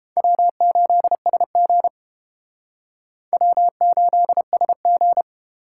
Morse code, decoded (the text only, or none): W8SG W8SG